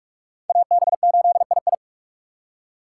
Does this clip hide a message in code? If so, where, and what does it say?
Yes: Morse code from 0.5 to 1.7 s, heard as a keyed tone, about 59 wpm, 690 Hz. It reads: AB8II